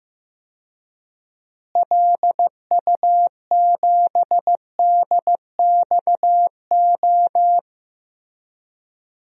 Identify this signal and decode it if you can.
Morse code, heard as a keyed tone, about 15 wpm, 700 Hz. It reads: LU7DXO
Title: DXCC-100 Call Signs - 6 Characters - Repeated 2x - 15wpm